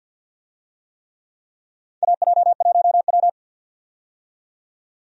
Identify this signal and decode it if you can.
Morse code, heard as a keyed tone, about 51 wpm, 700 Hz. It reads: AJ1W
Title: USA Call Signs - 4 Characters - 50wpm